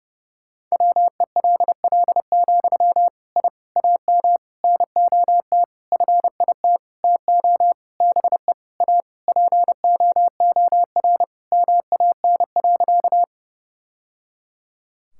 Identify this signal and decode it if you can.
Morse code, heard as a keyed tone, about 30 wpm, 700 Hz. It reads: WELL, I AM NOT FIT TO BE A POOR MAN.
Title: Sentences from Top 1000 Words - Review Part 5 - 30wpm